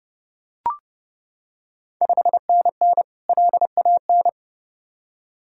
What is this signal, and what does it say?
Morse code, heard as a keyed tone, about 30 wpm, 700 Hz. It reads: 5NN LAN
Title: Ontario QSO Party Exchange 30wpm